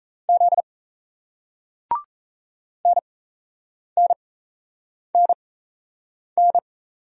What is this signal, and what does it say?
Morse code, heard as a keyed tone, about 35 wpm, spaced as for 8 wpm, 700 Hz. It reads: Z NNNN
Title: Speed-Racing Single Letters 28wpm